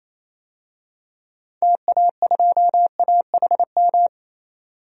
Morse code, heard as a keyed tone, about 28 wpm, 700 Hz. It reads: TA2AHM
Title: DXCC-100 Call Signs - 6 Characters - 28wpm